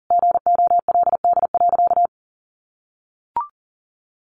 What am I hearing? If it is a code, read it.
Morse code, heard as a keyed tone, about 40 wpm, 700 Hz. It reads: GOLD.
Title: Sentences from Top 1500 Words 40wpm